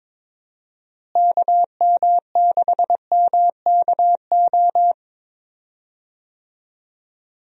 Morse code, heard as a keyed tone, about 22 wpm, 700 Hz. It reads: KM6MKO
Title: USA Call Signs - 6 Characters - Repeated 3x - 22wpm